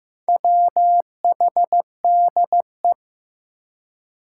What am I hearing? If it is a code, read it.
Morse code, heard as a keyed tone, about 15 wpm, 700 Hz. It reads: WHDE
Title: Four Challenging Letter-Number Combinations 15wpm